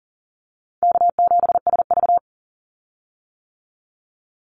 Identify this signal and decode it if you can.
Morse code, heard as a keyed tone, about 41 wpm, 700 Hz. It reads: K7SV
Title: Call Signs - 40wpm